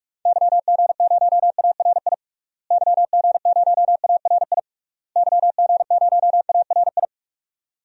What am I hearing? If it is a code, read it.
Morse code, heard as a keyed tone, about 46 wpm, 700 Hz. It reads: YG0ARI YG0ARI YG0ARI
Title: DXCC-100 Call Signs - 6 Characters - Repeated 3x - 45wpm